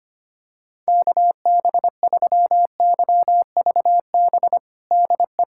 Morse code, heard as a keyed tone, about 25 wpm, 700 Hz. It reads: KB3YVB DE
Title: USA Callsign DE Callsign - 6 Characters - 25wpm